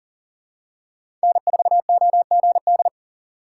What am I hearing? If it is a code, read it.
Morse code, heard as a keyed tone, about 40 wpm, 700 Hz. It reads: N4OGD
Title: USA Call Signs - 5 Characters - 2x Character Spacing - 40wpm